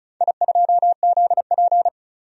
Morse code, heard as a keyed tone, about 36 wpm, 700 Hz. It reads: I2ZP
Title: DXCC-100 Call Signs - 4 Characters - 35wpm